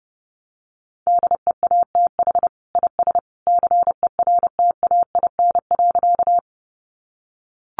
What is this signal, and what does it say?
Morse code, heard as a keyed tone, about 30 wpm, 700 Hz. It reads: DEATH IS CERTAIN.
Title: Sentences from Top 1100 Words 30wpm